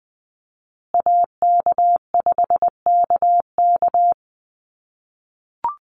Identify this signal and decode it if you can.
Morse code, heard as a keyed tone, about 20 wpm, 700 Hz. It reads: AK5KK
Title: USA Call Signs Repeated 2x 20wpm